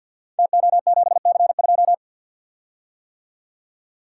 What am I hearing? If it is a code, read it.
Morse code, heard as a keyed tone, about 52 wpm, 700 Hz. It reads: TO7C2
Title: Code Groups - 5 Letter:Numbers - 50wpm